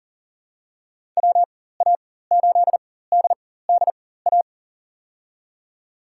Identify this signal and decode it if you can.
Morse code, heard as a keyed tone, about 40 wpm, spaced as for 19 wpm, 700 Hz. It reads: WA8DDA
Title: USA Call Signs - 6 Characters - 4x Character Spacing - 40wpm